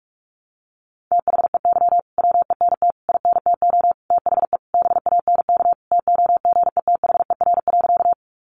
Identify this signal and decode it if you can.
Morse code, heard as a keyed tone, about 44 wpm, 700 Hz. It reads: THEY WENT INTO THE BANK TOGETHER.
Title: Sentences from Top 900 Words 45wpm